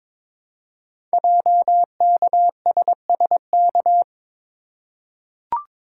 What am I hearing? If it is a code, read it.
Morse code, heard as a keyed tone, about 22 wpm, 700 Hz. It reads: JKSSK
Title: Code Groups - 5 Letter:Numbers - 22wpm